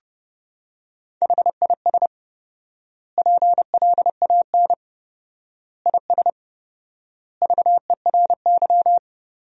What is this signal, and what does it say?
Morse code, heard as a keyed tone, about 30 wpm, 700 Hz. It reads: HIS PLAN IS VERY
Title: 4x Word-Spacing - Sentences from Top 500 Words - 30wpm